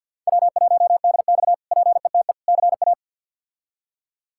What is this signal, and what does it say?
Morse code, heard as a keyed tone, about 48 wpm, 700 Hz. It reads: W1DX PETE CA
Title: K1USN SST QSO Practice - 50wpm